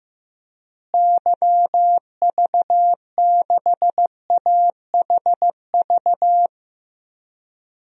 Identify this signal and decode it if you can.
Morse code, heard as a keyed tone, about 15 wpm, 700 Hz. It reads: YV6AHV
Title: DXCC-100 Call Signs - 6 Characters - Repeated 2x - 15wpm